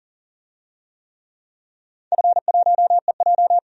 Morse code, heard as a keyed tone, about 40 wpm, 700 Hz. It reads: F1EJ
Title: DXCC-100 Call Signs - 4 Characters - 40wpm